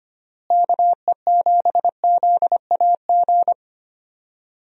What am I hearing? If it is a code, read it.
Morse code, heard as a keyed tone, about 25 wpm, 700 Hz. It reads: KE7ZAG
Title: USA Call Signs - 6 Characters - Repeated 3x - 25wpm